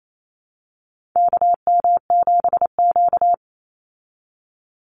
Morse code, heard as a keyed tone, about 28 wpm, 700 Hz. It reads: KM7Q
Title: Call Signs Repeated 2x - 28wpm